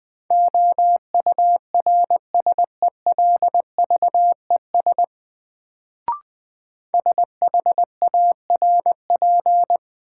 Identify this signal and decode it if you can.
Morse code, heard as a keyed tone, about 21 wpm, 700 Hz. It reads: OURSELVES SHARP
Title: Sentences from Top 1600 Words - New Word Drill Alt - 20wpm